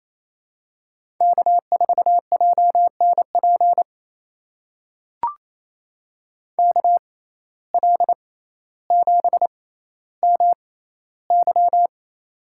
Morse code, heard as a keyed tone, about 28 wpm, spaced as for 14 wpm, 700 Hz. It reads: K4JNP KL7MY
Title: USA Call Signs - 5 Characters - 6x Character Spacing - 28wpm